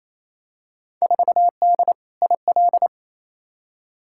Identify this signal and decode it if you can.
Morse code, heard as a keyed tone, about 30 wpm, 700 Hz. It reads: 4D IL
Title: Field Day Exchanges Encore - 28wpm